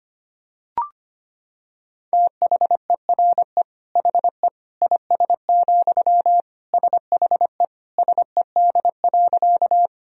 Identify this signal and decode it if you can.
Morse code, heard as a keyed tone, about 25 wpm, 700 Hz. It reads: THERE HE IS, SHE SED.
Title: Sentences from Top 100 Words and Abbreviations 25wpm